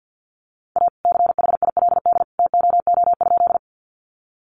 Morse code, heard as a keyed tone, about 50 wpm, 700 Hz. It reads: A CHILD TOO?
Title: Sentences from Top 800 Words 50wpm